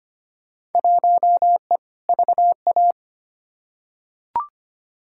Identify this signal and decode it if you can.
Morse code, heard as a keyed tone, about 25 wpm, 700 Hz. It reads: 1E VA
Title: Field Day Exchanges - 25wpm